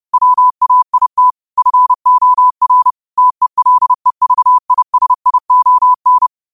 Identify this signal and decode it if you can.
Morse code, heard as a keyed tone, about 30 wpm, 1000 Hz. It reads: WAIT FOR TELEVISION
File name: qotdfiles / 30wpm